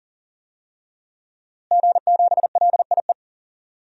Morse code, heard as a keyed tone, about 40 wpm, 700 Hz. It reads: G7LIE
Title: DXCC-100 Call Signs - 5 Characters - Repeated 2x - 40wpm